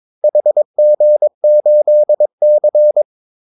Morse code, heard as a keyed tone, about 22 wpm, 600 Hz. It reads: HG8C
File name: Callsigns-22wpm